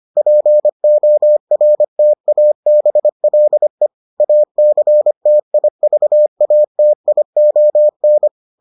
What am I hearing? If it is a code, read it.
Morse code, heard as a keyed tone, about 25 wpm, 600 Hz. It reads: PORTABLE ACTIVATION